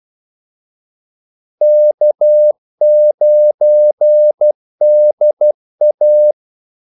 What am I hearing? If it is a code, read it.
Morse code, heard as a keyed tone, about 12 wpm, 600 Hz. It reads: K9DA
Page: 2 ekstra ordmellemrum mellem kaldesignalerne